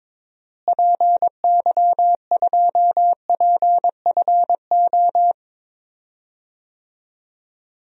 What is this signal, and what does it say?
Morse code, heard as a keyed tone, about 22 wpm, 700 Hz. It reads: PY2PFO
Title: DXCC-100 Call Signs - 6 Characters - Repeated 2x - 22wpm